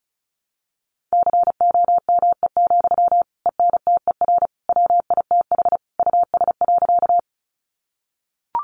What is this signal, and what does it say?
Morse code, heard as a keyed tone, about 34 wpm, 700 Hz. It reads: COME, ENTER WITH US.
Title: Sentences from Top 1300 Words 35wpm